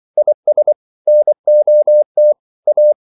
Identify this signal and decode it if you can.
Morse code, heard as a keyed tone, about 24 wpm, 600 Hz. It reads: IS NOT A